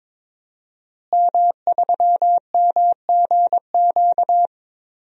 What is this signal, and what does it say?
Morse code, heard as a keyed tone, about 22 wpm, 700 Hz. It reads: M3MGQ
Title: DXCC-100 Call Signs - 5 Characters - Repeated 3x - 22wpm